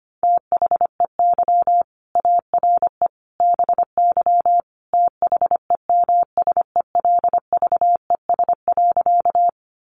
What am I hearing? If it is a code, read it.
Morse code, heard as a keyed tone, about 25 wpm, 700 Hz. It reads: THEY ARE BY THEMSELVES.
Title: Sentences from Top 800 Words 25wpm